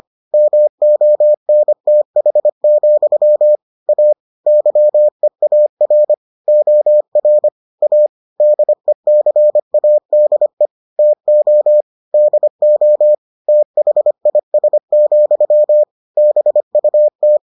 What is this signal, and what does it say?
Morse code, heard as a keyed tone, about 25 wpm, 600 Hz. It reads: MONTH, A YEAR OR A DECADE TO DO THIS, BUT